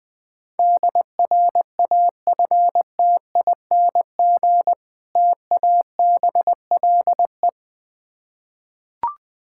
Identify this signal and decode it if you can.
Morse code, heard as a keyed tone, about 20 wpm, 690 Hz. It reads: DRAFTING TABLE
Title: Difficult Words with Hints - 20wpm